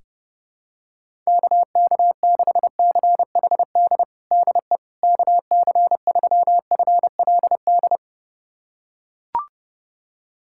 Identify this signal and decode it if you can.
Morse code, heard as a keyed tone, about 32 wpm, 700 Hz. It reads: KK6CHD DE KC3FLD
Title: USA Callsign DE Callsign - 6 Characters - 30wpm